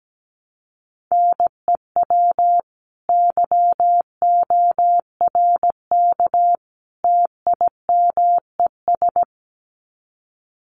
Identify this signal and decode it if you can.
Morse code, heard as a keyed tone, about 17 wpm, 700 Hz. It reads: NEW YORK TIMES
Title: Sets of 3 Words 17wpm